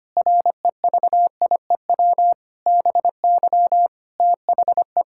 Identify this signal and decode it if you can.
Morse code, heard as a keyed tone, about 25 wpm, 700 Hz. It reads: REVIEW BY THE
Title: Sentences from Top 1000 Words - Review Part 6 - 25wpm